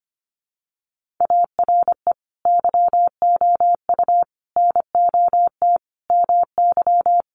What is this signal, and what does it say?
Morse code, heard as a keyed tone, about 25 wpm, 700 Hz. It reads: ARE YOU NOT MY
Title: Sentences from Top 1300 Words 25wpm